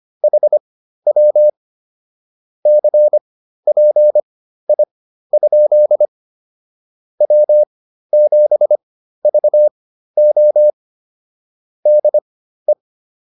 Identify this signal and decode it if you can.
Morse code, heard as a keyed tone, about 25 wpm, spaced as for 13 wpm, 600 Hz. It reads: HW CPI? W7VO DE